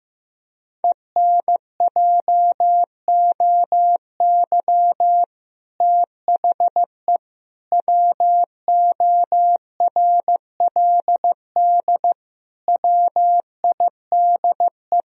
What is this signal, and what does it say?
Morse code, heard as a keyed tone, about 15 wpm, 700 Hz. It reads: ENJOY THE WORLD WIDE
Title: Sentences from Top 1000 Words - Review Part 4 - 15wpm